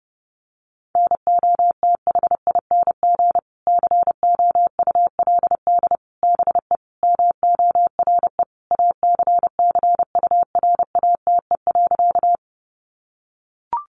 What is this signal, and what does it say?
Morse code, heard as a keyed tone, about 30 wpm, 700 Hz. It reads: NOTHING COULD BE MORE ACCURATE.
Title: Sentences from Top 1200 Words 30wpm